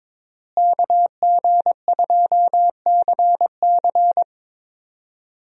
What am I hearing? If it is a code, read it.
Morse code, heard as a keyed tone, about 22 wpm, 700 Hz. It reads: KG2CC
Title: USA Call Signs - 5 Characters - Repeated 2x - 22wpm